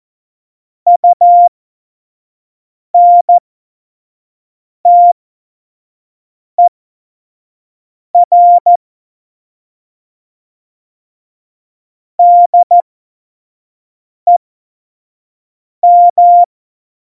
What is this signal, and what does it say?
Morse code, heard as a keyed tone, about 14 wpm, spaced as for 5 wpm, 700 Hz. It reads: UNTER DEM
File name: Morsekurs / Witze / Witz 29. Oktober 2019, 25 F70